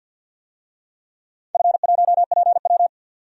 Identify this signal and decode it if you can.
Morse code, heard as a keyed tone, about 46 wpm, 700 Hz. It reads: F1PW
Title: DXCC-100 Call Signs - 4 Characters - 50wpm